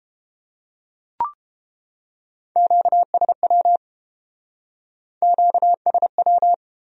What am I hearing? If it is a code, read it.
Morse code, heard as a keyed tone, about 31 wpm, 700 Hz. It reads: QSW QSW